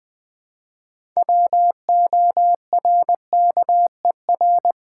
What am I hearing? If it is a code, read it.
Morse code, heard as a keyed tone, about 20 wpm, 700 Hz. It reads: WORKER
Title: Six Letter Words - 20wpm